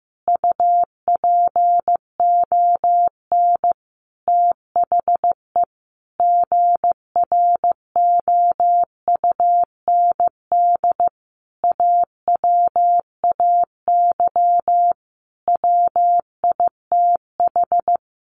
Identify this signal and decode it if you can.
Morse code, heard as a keyed tone, about 15 wpm, 700 Hz. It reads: UPON THE GROUND AWAY WITH
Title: Sentences from Top 1100 Words 15wpm